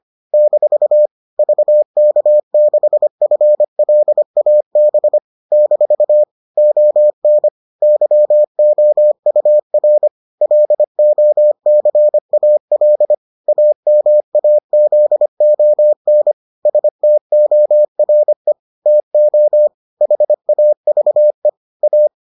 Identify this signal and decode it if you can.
Morse code, heard as a keyed tone, about 25 wpm, 600 Hz. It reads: - VK6FLAB - ON YOUR LOCAL AMAZON STORE TO HAVE A